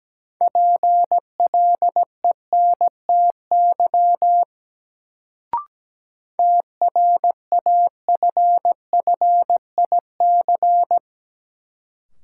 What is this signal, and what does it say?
Morse code, heard as a keyed tone, about 17 wpm, 700 Hz. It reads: PLENTY TRAFFIC
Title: Sentences from Top 1300 Words - New Word Drill - 17wpm